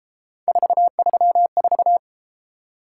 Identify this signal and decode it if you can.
Morse code, heard as a keyed tone, about 32 wpm, 700 Hz. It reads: 434